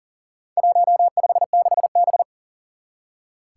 Morse code, heard as a keyed tone, about 40 wpm, 700 Hz. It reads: 156B
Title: Four Challenging Letter-Number Combinations 40wpm